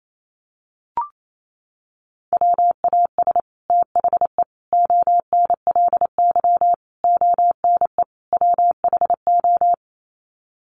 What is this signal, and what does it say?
Morse code, heard as a keyed tone, about 28 wpm, 700 Hz. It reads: WAS THE ONLY ONE WHO
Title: Sets of 5 Words Encore2 28wpm